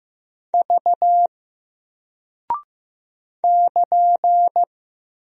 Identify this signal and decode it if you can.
Morse code, heard as a keyed tone, about 15 wpm, 700 Hz. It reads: V <KN>
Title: Single Character (Letter:Number:Punctuation:Prosign) Alt 15wpm